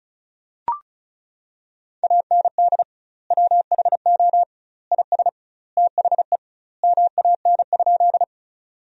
Morse code, heard as a keyed tone, about 36 wpm, 700 Hz. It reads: AND WHO IS THE MAN?
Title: 2x Word-Spacing - Sentences from Top 200 Words - 35wpm